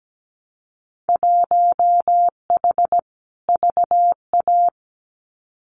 Morse code, heard as a keyed tone, about 17 wpm, 700 Hz. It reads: 1H VA
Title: Winter Field Day Exchanges 17wpm